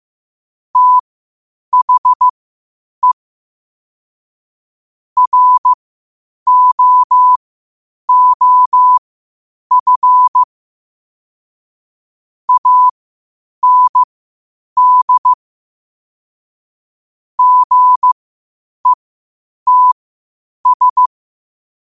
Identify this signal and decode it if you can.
Morse code, heard as a keyed tone, about 15 wpm, spaced as for 8 wpm, 1000 Hz. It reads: THE ROOF AND GETS
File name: qotdfiles / 8wpm